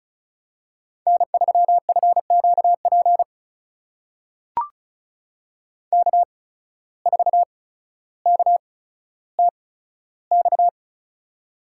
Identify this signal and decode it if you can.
Morse code, heard as a keyed tone, about 36 wpm, spaced as for 15 wpm, 700 Hz. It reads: N3FQP K4KTX
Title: USA Call Signs - 5 Characters - 8x Character Spacing - 35wpm